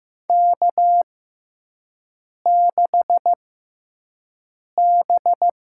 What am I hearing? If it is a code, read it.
Morse code, heard as a keyed tone, about 15 wpm, spaced as for 5 wpm, 700 Hz. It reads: K6B
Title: USA Call Signs - 5 Characters - 6x Character Spacing - 15wpm